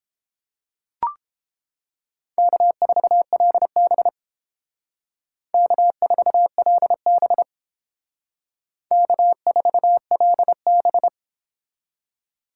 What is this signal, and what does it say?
Morse code, heard as a keyed tone, about 31 wpm, 700 Hz. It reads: K4LB K4LB K4LB